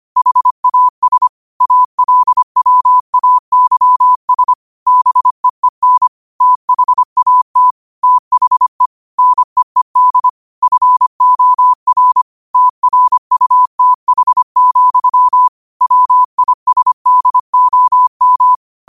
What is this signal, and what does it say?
Morse code, heard as a keyed tone, about 25 wpm, 1000 Hz. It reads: HAS ALWAYS BEEN THAT THE NEED FOR TRUTH, WISDOM
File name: 25wpm